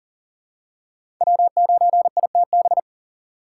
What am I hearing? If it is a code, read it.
Morse code, heard as a keyed tone, about 40 wpm, 700 Hz. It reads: W9ITB
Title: USA Call Signs - 4x Character Spacing - 40wpm